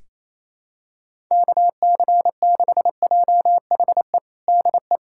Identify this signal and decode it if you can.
Morse code, heard as a keyed tone, about 27 wpm, 700 Hz. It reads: KC6JHE DE
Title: USA Callsign DE Callsign 28wpm